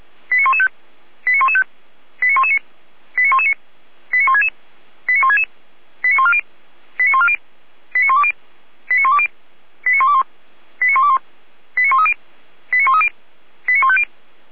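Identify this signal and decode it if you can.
Five-tone selective call (ZVEI), heard as quick runs of five tones, tones of about 70 ms. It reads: 89115 89115 89119 89119 89166 89166 89140 89140 89120 89120 89121 89121 89130 89130 89150
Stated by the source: ZVEI five-tone call received by SDR.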